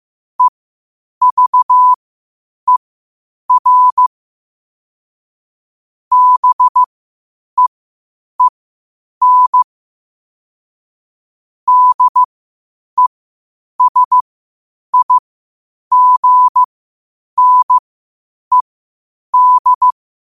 Morse code, heard as a keyed tone, about 15 wpm, spaced as for 8 wpm, 1000 Hz. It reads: EVER BEEN DESIGNED
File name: qotdfiles / 8wpm